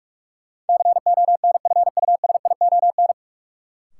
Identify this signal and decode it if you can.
Morse code, heard as a keyed tone, about 47 wpm, 700 Hz. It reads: CONFUSION
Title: Sentences from Top 1600 Words - New Word Drill - 45wpm